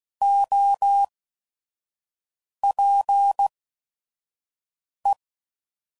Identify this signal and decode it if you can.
Morse code, heard as a keyed tone, about 16 wpm, 800 Hz. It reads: O P E